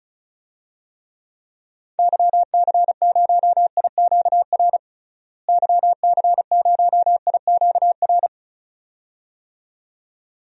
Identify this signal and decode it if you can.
Morse code, heard as a keyed tone, about 34 wpm, 700 Hz. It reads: YC0IQR YC0IQR
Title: DXCC-100 Call Signs - 6 Characters - Repeated 2x - 35wpm